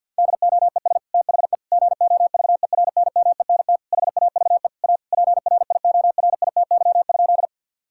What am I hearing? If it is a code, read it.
Morse code, heard as a keyed tone, about 50 wpm, 700 Hz. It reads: DOES THE GOVERNMENT HAVE A PRIORITY?
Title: Sentences from Top 1600 Words 50wpm